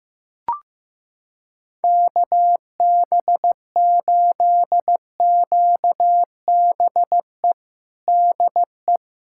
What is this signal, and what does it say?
Morse code, heard as a keyed tone, about 15 wpm, 700 Hz. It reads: KB8QBE DE
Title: USA Callsign DE Callsign 15wpm